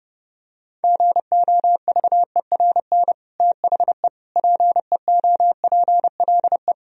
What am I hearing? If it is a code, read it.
Morse code, heard as a keyed tone, about 30 wpm, 700 Hz. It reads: GOVERN THE PEOPLE
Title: Sentences from Top 1000 Words - Review Part 2 - 30wpm